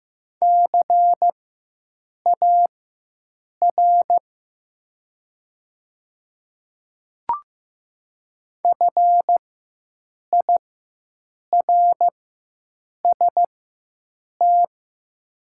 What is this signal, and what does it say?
Morse code, heard as a keyed tone, about 15 wpm, spaced as for 7 wpm, 700 Hz. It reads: CAR FIRST
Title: 4x Character Spacing - Top 300 Words - 15wpm